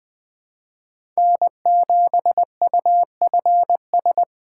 Morse code, heard as a keyed tone, about 21 wpm, 700 Hz. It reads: N7UFS
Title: USA Call Signs - 5 Characters - 3x Character Spacing - 20wpm